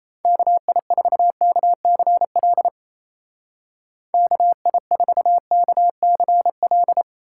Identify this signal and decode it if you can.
Morse code, heard as a keyed tone, about 34 wpm, 700 Hz. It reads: KI4KCL KI4KCL
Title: Speed Racing - USA Call Signs 28wpm